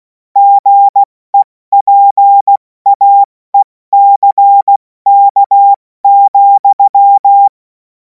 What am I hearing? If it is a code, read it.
Morse code, heard as a keyed tone, about 16 wpm, 800 Hz. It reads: GEPAECK,
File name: Morsekurs / Witze / Witz 31. März 2026 70 F80